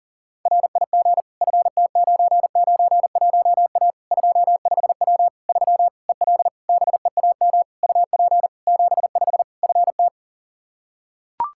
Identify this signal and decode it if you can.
Morse code, heard as a keyed tone, about 40 wpm, 700 Hz. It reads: RIG FT991A 25W 3 EL BEAM UP 75 FT
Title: K1USN SST QSO Practice 40wpm